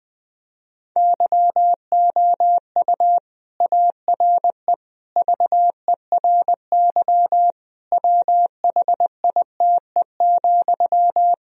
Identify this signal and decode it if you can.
Morse code, heard as a keyed tone, about 21 wpm, 700 Hz. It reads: YOU ARE VERY WHITE,
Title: Sentences from Top 1000 Words - Review Part 1 - 20wpm